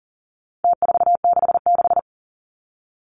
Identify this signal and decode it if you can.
Morse code, heard as a keyed tone, about 40 wpm, 700 Hz. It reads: T466
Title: Numbers - Contest Style - 4-Digit 40wpm